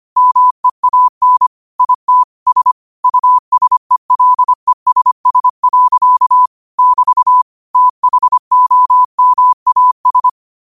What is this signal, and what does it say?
Morse code, heard as a keyed tone, about 25 wpm, 1000 Hz. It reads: MEAN IT S USELESS. <BT> THOMAS
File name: qotdfiles / 25wpm